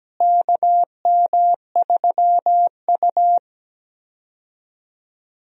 Morse code, heard as a keyed tone, about 17 wpm, 700 Hz. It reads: KM3U
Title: USA Call Signs - 4 Characters - Repeated 2x - 17wpm